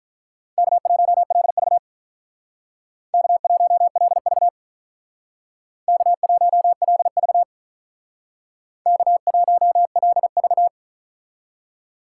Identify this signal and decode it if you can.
Morse code, heard as a keyed tone, about 47 wpm, 700 Hz. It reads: K1LV K1LV K1LV K1LV